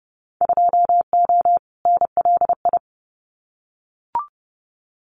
Morse code, heard as a keyed tone, about 30 wpm, 700 Hz. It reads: 2O NLI
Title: Winter Field Day Exchanges 30wpm